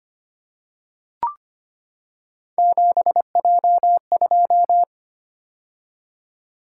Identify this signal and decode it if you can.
Morse code, heard as a keyed tone, about 25 wpm, 700 Hz. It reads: 7J2
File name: Three Letter-Number Combinations - 25wpm